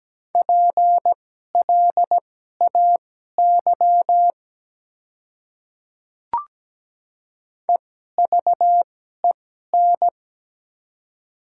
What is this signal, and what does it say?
Morse code, heard as a keyed tone, about 17 wpm, spaced as for 12 wpm, 700 Hz. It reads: PLAY EVEN
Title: 2x Character Spacing - Top 300 Words - 17wpm